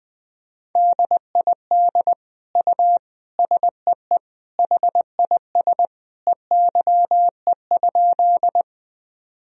Morse code, heard as a keyed tone, about 20 wpm, 700 Hz. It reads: DID U SEE HIS EYE?
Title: Sentences from Top 300 Words and Abbreviations 20wpm